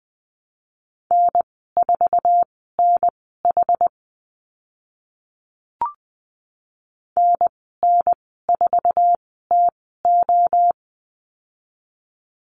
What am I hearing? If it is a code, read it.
Morse code, heard as a keyed tone, about 20 wpm, spaced as for 14 wpm, 700 Hz. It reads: N4NH NN4TO
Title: 082 Lesson 14g - Number 2 - Call Signs 20wpm